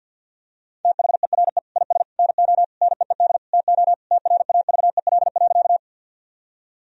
Morse code, heard as a keyed tone, about 50 wpm, 700 Hz. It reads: THERE IS NO NEED TO TRAVEL.
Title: Sentences from Top 1000 Words - Review Part 2 - 50wpm